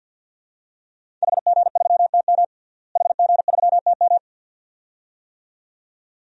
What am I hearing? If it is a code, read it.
Morse code, heard as a keyed tone, about 50 wpm, 700 Hz. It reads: HG3TM HG3TM
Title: DXCC-100 Call Signs - 5 Characters - Repeated 2x - 50wpm